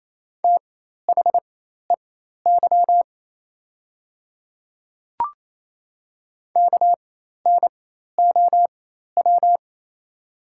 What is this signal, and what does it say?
Morse code, heard as a keyed tone, about 28 wpm, spaced as for 13 wpm, 700 Hz. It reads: THEY KNOW
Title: 4x Character Spacing - Top 300 Words - 28wpm